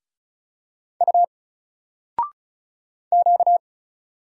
Morse code, heard as a keyed tone, about 36 wpm, 700 Hz. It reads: U Q
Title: Single Letter-Number Alt 35wpm